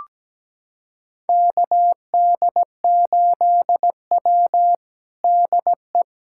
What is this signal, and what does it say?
Morse code, heard as a keyed tone, about 17 wpm, 700 Hz. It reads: KD8W DE
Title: USA Callsign DE Callsign - 4 Characters - 17wpm